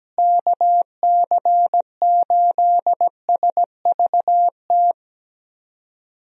Morse code, heard as a keyed tone, about 17 wpm, 700 Hz. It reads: KC8SVT
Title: USA Call Signs - 6 Characters - Repeated 3x - 17wpm